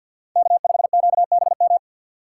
Morse code, heard as a keyed tone, about 52 wpm, 700 Hz. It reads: K5QBM
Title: USA Call Signs - 5 Characters - 2x Character Spacing - 50wpm